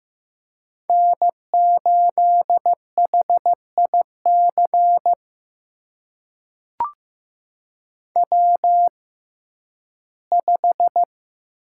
Morse code, heard as a keyed tone, about 15 wpm, spaced as for 11 wpm, 700 Hz. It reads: N8HIC W5
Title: USA Call Signs - 5 Characters - 6x Character Spacing - 15wpm